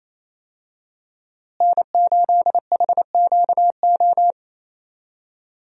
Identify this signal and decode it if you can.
Morse code, heard as a keyed tone, about 27 wpm, 700 Hz. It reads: N8HQO
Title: USA Call Signs - 5 Characters - Repeated 3x - 28wpm